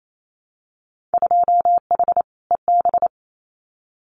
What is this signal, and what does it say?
Morse code, heard as a keyed tone, about 28 wpm, 700 Hz. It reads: 2H EB
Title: Winter Field Day Exchanges Encore 28wpm